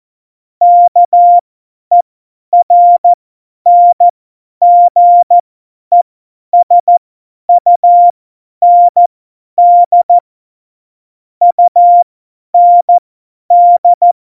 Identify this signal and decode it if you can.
Morse code, heard as a keyed tone, about 14 wpm, spaced as for 10 wpm, 700 Hz. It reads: KERNGESUND UND
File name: Morsekurs / Witze / Witz 08. Oktober 2019, 50 F70